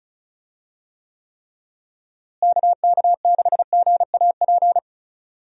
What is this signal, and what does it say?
Morse code, heard as a keyed tone, about 36 wpm, 700 Hz. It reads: KK6GAP
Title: USA Call Signs - 6 Characters - Repeated 3x - 35wpm